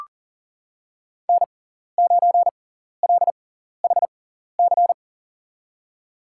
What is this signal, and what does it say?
Morse code, heard as a keyed tone, about 40 wpm, spaced as for 14 wpm, 700 Hz. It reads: N9LHC
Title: USA Call Signs - 5 Characters - 6x Character Spacing - 40wpm